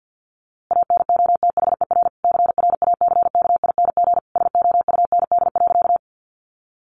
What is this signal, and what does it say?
Morse code, heard as a keyed tone, about 48 wpm, 700 Hz. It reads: ANOTHER CRACKING SOUND.
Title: A Sound Of Thunder 50wpm